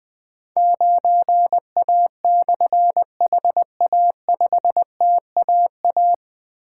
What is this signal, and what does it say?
Morse code, heard as a keyed tone, about 20 wpm, 700 Hz. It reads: 9A/HA5TAA
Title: Difficult Call Signs 20wpm